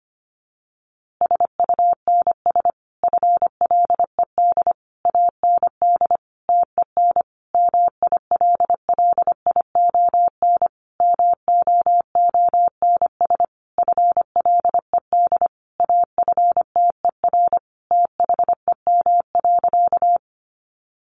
Morse code, heard as a keyed tone, about 25 wpm, 700 Hz. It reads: SUNS FLED AND TEN MILLION MOONS FLED AFTER THEM.
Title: A Sound Of Thunder 25wpm